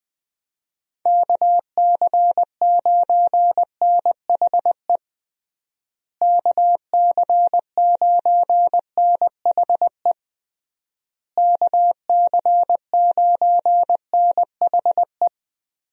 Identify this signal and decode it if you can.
Morse code, heard as a keyed tone, about 20 wpm, 700 Hz. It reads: KC9NHE KC9NHE KC9NHE
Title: USA Call Signs - 6 Characters - Repeated 3x - 20wpm